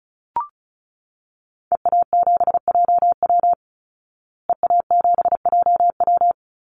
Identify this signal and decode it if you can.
Morse code, heard as a keyed tone, about 34 wpm, 700 Hz. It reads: EA7JW EA7JW
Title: Call Signs Repeated 2x - 35wpm